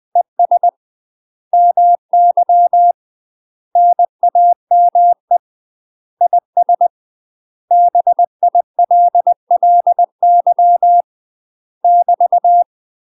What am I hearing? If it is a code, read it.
Morse code, heard as a keyed tone, about 20 wpm, 700 Hz. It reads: ES MY NAME IS BILLY <BT>
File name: UK MASTER QSOs TRAINING / 20WPM / UK MASTER QSOs 0013